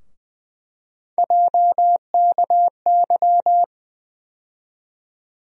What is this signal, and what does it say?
Morse code, heard as a keyed tone, about 20 wpm, 700 Hz. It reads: JKY
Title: USA Call Sign Postfixes - 3 Letters 20wpm